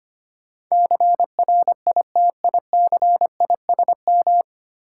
Morse code, heard as a keyed tone, about 25 wpm, 700 Hz. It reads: CRITICISM
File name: 9 Letter Words - First 3 Letters Given - 25wpm